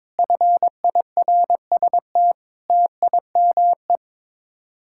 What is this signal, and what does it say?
Morse code, heard as a keyed tone, about 22 wpm, 700 Hz. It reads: FIRST TIME
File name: practice / 22wpm / Build A Phrase - 5-Word Phrases - 22wpm